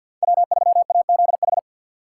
Morse code, heard as a keyed tone, about 48 wpm, 700 Hz. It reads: W3AZH
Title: USA Call Signs - 5 Characters - 2x Character Spacing - 50wpm